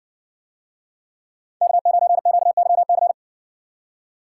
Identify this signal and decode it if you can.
Morse code, heard as a keyed tone, about 60 wpm, 700 Hz. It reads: K9QYC